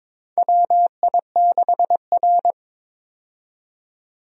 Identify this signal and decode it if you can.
Morse code, heard as a keyed tone, about 22 wpm, 700 Hz. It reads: WI6R
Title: USA Call Signs - 4 Characters - 22wpm